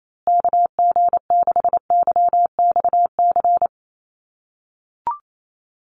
Morse code, heard as a keyed tone, about 28 wpm, 700 Hz. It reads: KG6YXC
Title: USA Call Signs Repeated 2x 28wpm